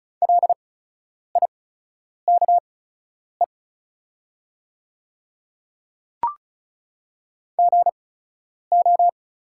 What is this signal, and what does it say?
Morse code, heard as a keyed tone, about 36 wpm, spaced as for 10 wpm, 700 Hz. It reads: LIKE GO
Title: 8x Character Spacing - Top 100 Words - 35wpm